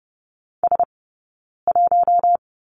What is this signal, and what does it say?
Morse code, heard as a keyed tone, about 30 wpm, 700 Hz. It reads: S 1
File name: Lesson 16d - Letter c - Two Characters 30wpm